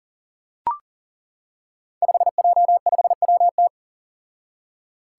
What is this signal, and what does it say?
Morse code, heard as a keyed tone, about 40 wpm, 700 Hz. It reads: 5J5WT
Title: Five Challenging Letter-Number Combinations 40wpm